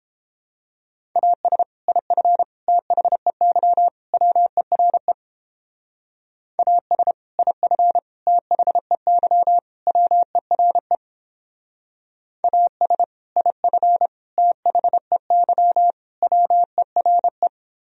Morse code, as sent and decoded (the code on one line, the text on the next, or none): .- ... / .. ..-. / - .... -.-- / .-- . .-. / .- ... / .. ..-. / - .... -.-- / .-- . .-. / .- ... / .. ..-. / - .... -.-- / .-- . .-. .
AS IF THEY WERE AS IF THEY WERE AS IF THEY WERE